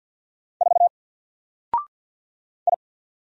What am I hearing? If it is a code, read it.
Morse code, heard as a keyed tone, about 48 wpm, 700 Hz. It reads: 4 I
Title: Single Letter-Number Alt 50wpm